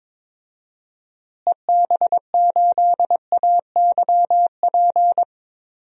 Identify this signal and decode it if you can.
Morse code, heard as a keyed tone, about 22 wpm, 700 Hz. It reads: EB8AYP
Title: DXCC-100 Call Signs - 6 Characters - Repeated 3x - 22wpm